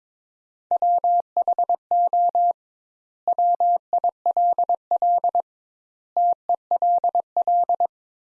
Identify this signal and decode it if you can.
Morse code, heard as a keyed tone, about 21 wpm, 700 Hz. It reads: WHO WILL TELL
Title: Sentences Top 200 Words - 22wpm